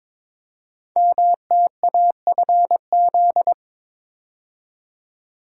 Morse code, heard as a keyed tone, about 22 wpm, 700 Hz. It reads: MTAFZ
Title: Code Groups - 5 Letters - 22wpm